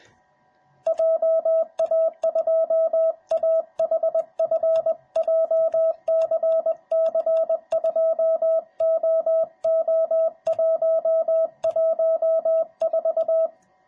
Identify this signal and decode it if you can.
Morse code, heard as a keyed tone, about 20 wpm, 650 Hz, with radio noise. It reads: JA2AHF JCC2OO114